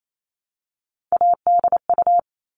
Morse code, heard as a keyed tone, about 28 wpm, 700 Hz. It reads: ADU
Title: All Cut Numbers - 3-Digit 28wpm